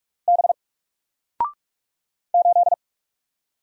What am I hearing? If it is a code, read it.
Morse code, heard as a keyed tone, about 44 wpm, 700 Hz. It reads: B 8
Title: Single Character (Letter:Number:Punctuation:Prosign) 45wpm